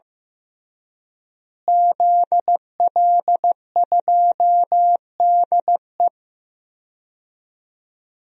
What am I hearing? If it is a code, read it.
Morse code, heard as a keyed tone, about 15 wpm, 700 Hz. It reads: ZL2DE
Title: DXCC-100 Call Signs - 5 Characters - Repeated 3x - 15wpm